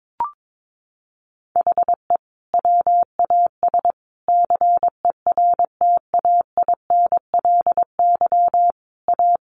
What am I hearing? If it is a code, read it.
Morse code, heard as a keyed tone, about 22 wpm, 700 Hz. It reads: HE WAS CERTAINLY A
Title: Sentences from Top 1200 Words 22wpm